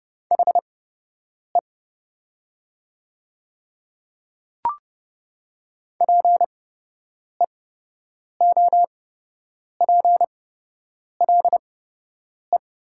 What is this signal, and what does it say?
Morse code, heard as a keyed tone, about 30 wpm, spaced as for 8 wpm, 700 Hz. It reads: HE PEOPLE
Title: 8x Character Spacing - Top 100 Words - 30wpm